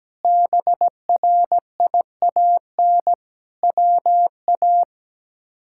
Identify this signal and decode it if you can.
Morse code, heard as a keyed tone, about 17 wpm, 700 Hz. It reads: BRIAN WA
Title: K1USN SST QSO Practice - 17wpm